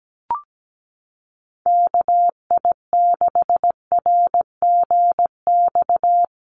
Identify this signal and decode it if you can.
Morse code, heard as a keyed tone, about 17 wpm, 700 Hz. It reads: KI6RGX
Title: USA Call Signs Repeated 2x 17wpm